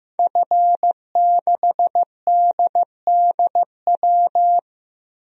Morse code, heard as a keyed tone, about 16 wpm, 700 Hz. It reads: F6DDW
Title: DXCC-100 Call Signs - 5 Characters - Repeated 2x - 15wpm